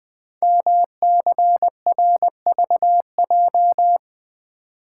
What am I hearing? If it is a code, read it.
Morse code, heard as a keyed tone, about 20 wpm, 700 Hz. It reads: MCRVJ
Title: Code Groups - 5 Letters - 20wpm